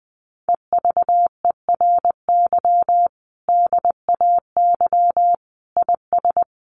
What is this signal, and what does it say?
Morse code, heard as a keyed tone, about 20 wpm, 700 Hz. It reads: EVERY DAY IS
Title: Sentences from Top 1400 Words 20wpm